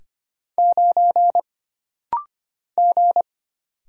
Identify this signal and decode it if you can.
Morse code, heard as a keyed tone, about 25 wpm, 700 Hz. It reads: 9 G
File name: Single Character (Letter:Number:Punctuation:Prosign) - Rapid-Fire - 25wpm